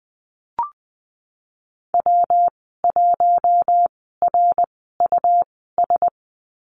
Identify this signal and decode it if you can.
Morse code, heard as a keyed tone, about 20 wpm, spaced as for 15 wpm, 700 Hz. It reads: W1RUS
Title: 122 Lesson 20g - Number 3 - Call Signs 20wpm